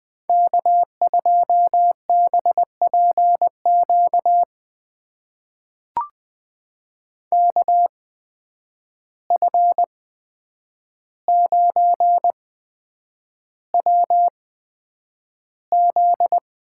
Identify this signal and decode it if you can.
Morse code, heard as a keyed tone, about 20 wpm, spaced as for 9 wpm, 700 Hz. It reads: K2BPQ KF9WZ
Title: USA Call Signs - 5 Characters - 8x Character Spacing - 20wpm